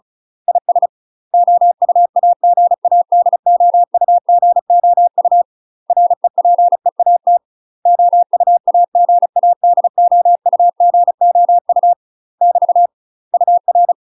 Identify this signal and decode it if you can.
Morse code, heard as a keyed tone, about 35 wpm, 700 Hz. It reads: IS OUAGADOUGOU REPEAT OUAGADOUGOU <BT> UR
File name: UK MASTER QSOs TRAINING / 35WPM / UK MASTER QSOs 0020